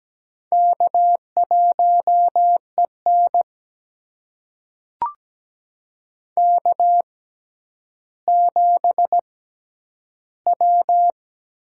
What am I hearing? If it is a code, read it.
Morse code, heard as a keyed tone, about 17 wpm, spaced as for 10 wpm, 700 Hz. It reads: K1EN K7W
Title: USA Call Signs - 6x Character Spacing - 17wpm